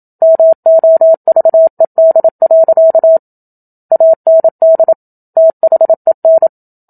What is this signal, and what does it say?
Morse code, heard as a keyed tone, about 27 wpm, 650 Hz. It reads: MOVED. AND THEN
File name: Book Two / 01-Chapter 1 - 27 WPM